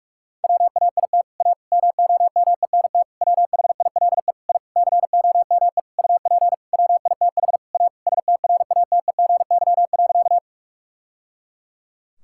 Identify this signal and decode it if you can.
Morse code, heard as a keyed tone, about 46 wpm, 700 Hz. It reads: WAIT A MOMENT WHILE I COME UP WITH A STRATEGY.
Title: Sentences from Top 1000 Words - Review Part 5 - 45wpm